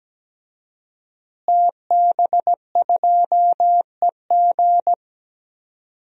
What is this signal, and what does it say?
Morse code, heard as a keyed tone, about 17 wpm, 700 Hz. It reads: TB2EG
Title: DXCC-100 Call Signs - 5 Characters - Repeated 3x - 17wpm